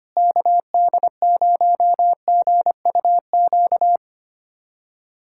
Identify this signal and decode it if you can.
Morse code, heard as a keyed tone, about 25 wpm, 700 Hz. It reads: KD0GUQ
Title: USA Callsign DE Callsign 25wpm